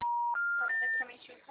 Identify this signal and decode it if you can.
Special information tone, with speech.
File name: out- -2009- - - .546